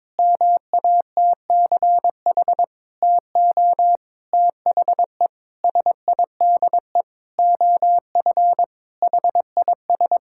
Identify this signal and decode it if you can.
Morse code, heard as a keyed tone, about 22 wpm, 700 Hz. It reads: MATCH TO THE SIDE OF HIS
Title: Sentences from Top 1600 Words 22wpm